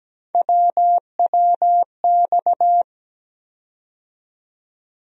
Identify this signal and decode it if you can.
Morse code, heard as a keyed tone, about 17 wpm, 700 Hz. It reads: WWX
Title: USA Call Sign Postfixes - 3 Letters 17wpm